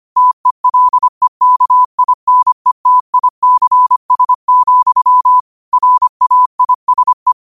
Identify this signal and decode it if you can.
Morse code, heard as a keyed tone, about 25 wpm, 1000 Hz. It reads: TELEKINETICS, RAISE
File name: qotdfiles / 25wpm